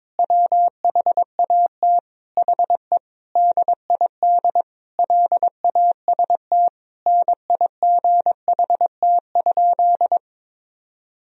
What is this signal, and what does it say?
Morse code, heard as a keyed tone, about 22 wpm, 700 Hz. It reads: WHAT HE DID LAST NIGHT?
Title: Sentences from Top 300 Words and Abbreviations 22wpm